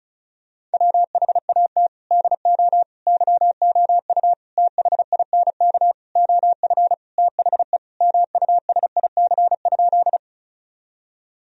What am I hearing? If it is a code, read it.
Morse code, heard as a keyed tone, about 36 wpm, 700 Hz. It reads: WHAT DO YOU THINK OF THE MUSIC?
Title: Sentences from Top 1000 Words - Review Part 1 - 35wpm